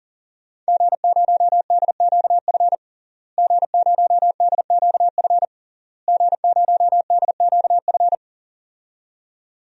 Morse code, heard as a keyed tone, about 37 wpm, 700 Hz. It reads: G0DQF G0DQF G0DQF
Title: DXCC-100 Call Signs - 5 Characters - Repeated 3x - 40wpm